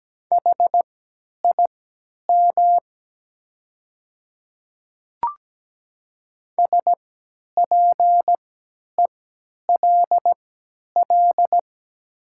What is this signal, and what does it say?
Morse code, heard as a keyed tone, about 17 wpm, spaced as for 10 wpm, 700 Hz. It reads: HIM SPELL
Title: 3x Character Spacing - Top 300 Words - 17wpm